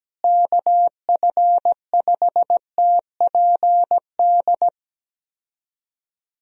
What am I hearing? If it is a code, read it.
Morse code, heard as a keyed tone, about 17 wpm, 700 Hz. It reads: KF5TPD
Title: USA Call Signs - 6 Characters - Repeated 2x - 17wpm